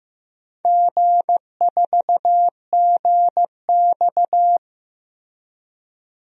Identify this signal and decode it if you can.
Morse code, heard as a keyed tone, about 15 wpm, 700 Hz. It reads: G4GX
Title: DXCC-100 Call Signs - 4 Characters - Repeated 3x - 15wpm